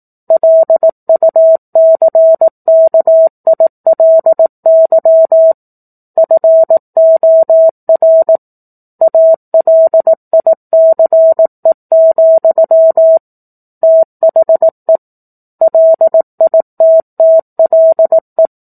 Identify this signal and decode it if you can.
Morse code, heard as a keyed tone, about 18 wpm, 650 Hz. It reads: LUCKILY FOR ALICE, THE LITTLE
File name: 04-Chapter-04 18 WPM